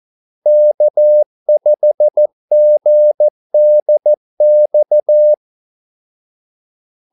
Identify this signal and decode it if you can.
Morse code, heard as a keyed tone, about 14 wpm, 600 Hz. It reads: K5GDX
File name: Callsigns-14wpm